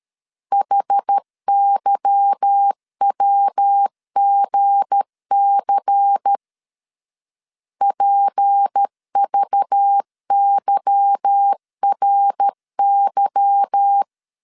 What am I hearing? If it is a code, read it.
Morse code, heard as a keyed tone, about 13 wpm, 800 Hz. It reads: HYWGC PVYRY
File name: LEZIONI / 8 YJ45 / 8 Y 12wpm 2228